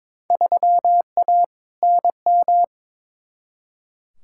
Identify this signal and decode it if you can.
Morse code, heard as a keyed tone, about 22 wpm, 700 Hz. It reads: 3A NM
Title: Field Day Exchanges Encore - 22wpm